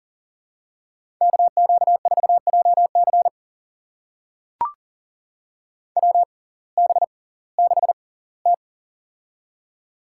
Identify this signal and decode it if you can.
Morse code, heard as a keyed tone, about 40 wpm, spaced as for 22 wpm, 700 Hz. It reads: KQ4JC WB6T
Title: USA Call Signs - 6x Character Spacing - 40wpm